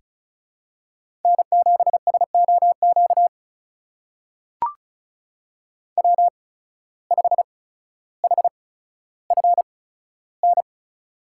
Morse code, heard as a keyed tone, about 36 wpm, spaced as for 15 wpm, 700 Hz. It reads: N7SOQ W5HFN
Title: USA Call Signs - 8x Character Spacing - 35wpm